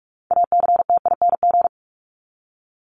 Morse code, heard as a keyed tone, about 44 wpm, 700 Hz. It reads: ACTING
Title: ING Suffix - Top 100 Words with Base Words 45wpm